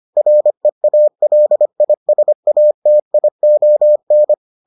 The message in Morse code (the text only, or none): REALISATION